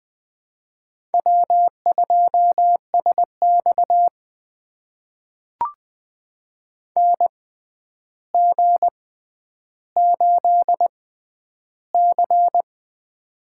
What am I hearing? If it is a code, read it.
Morse code, heard as a keyed tone, about 20 wpm, spaced as for 10 wpm, 700 Hz. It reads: W2SX NG8C
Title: USA Call Signs - 4 Characters - 6x Character Spacing - 20wpm